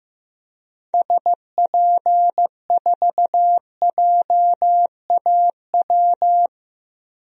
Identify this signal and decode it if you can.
Morse code, heard as a keyed tone, about 15 wpm, 700 Hz. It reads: SP4JAW
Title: DXCC-100 Call Signs - 6 Characters - 15wpm